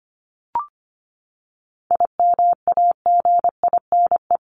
Morse code, heard as a keyed tone, about 25 wpm, 700 Hz. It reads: IMAGINE
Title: Top 1251-1500 Words 25wpm